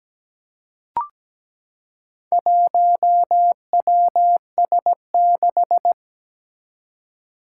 Morse code, heard as a keyed tone, about 17 wpm, 700 Hz. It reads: 1WS6
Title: Four Challenging Letter-Number Combinations 17wpm